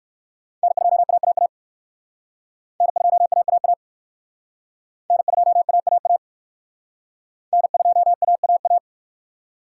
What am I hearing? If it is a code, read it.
Morse code, heard as a keyed tone, about 54 wpm, 700 Hz. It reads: N2AAA N2AAA N2AAA N2AAA
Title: Speed Racing - USA 1x3 Call Signs 45wpm